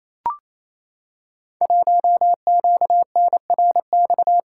Morse code, heard as a keyed tone, about 28 wpm, 690 Hz. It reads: 1QNRX
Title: Code Groups - 5 Letter:Numbers - 28wpm